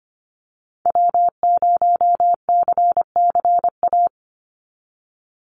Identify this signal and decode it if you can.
Morse code, heard as a keyed tone, about 25 wpm, 700 Hz. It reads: W0CCA
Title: NA SOTA Activator Callsigns 25wpm